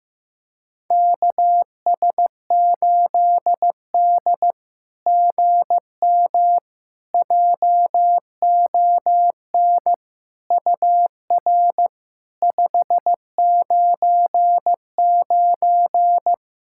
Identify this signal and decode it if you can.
Morse code, heard as a keyed tone, about 15 wpm, 700 Hz. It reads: KS8D GM JON UR 599
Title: SOTA QSO Practice 15wpm